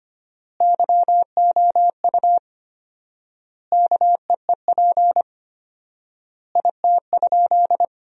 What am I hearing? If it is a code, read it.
Morse code, heard as a keyed tone, about 25 wpm, 700 Hz. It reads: YOU KEEP IT?
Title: 4x Word-Spacing - Sentences from Top 300 Words - 25wpm